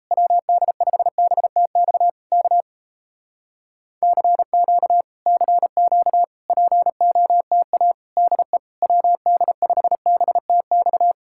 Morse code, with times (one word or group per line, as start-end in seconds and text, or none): 0.1-2.1 WD5BTX
2.3-2.6 K
4.0-5.0 CQ
5.3-6.2 CQ
6.5-7.9 POTA
8.2-8.6 DE
8.8-11.1 WD5BTX